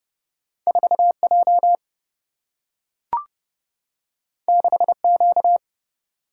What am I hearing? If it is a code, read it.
Morse code, heard as a keyed tone, about 30 wpm, 700 Hz. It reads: 4J 6Q
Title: Single-Letter Single-Number 30wpm